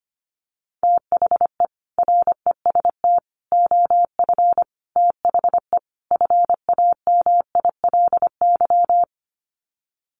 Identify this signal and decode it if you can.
Morse code, heard as a keyed tone, about 25 wpm, 700 Hz. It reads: THE REST OF THE FAMILY
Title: Sets of 5 Words Encore2 25wpm